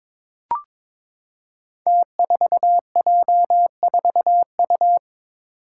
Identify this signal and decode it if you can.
Morse code, heard as a keyed tone, about 22 wpm, 700 Hz. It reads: T4J4U
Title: Five Challenging Letter-Number Combinations 22wpm